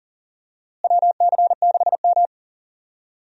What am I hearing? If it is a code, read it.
Morse code, heard as a keyed tone, about 37 wpm, 700 Hz. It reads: WC6M
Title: USA Call Signs - 4 Characters - Repeated 2x - 40wpm